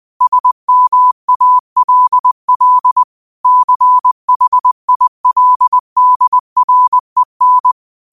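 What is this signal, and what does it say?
Morse code, heard as a keyed tone, about 20 wpm, 1000 Hz. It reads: SMALL CHILDREN
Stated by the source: Morse code quote of the day at 20 wpm.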